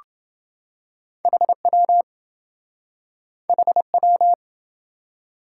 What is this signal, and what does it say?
Morse code, heard as a keyed tone, about 27 wpm, 700 Hz. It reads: HW HW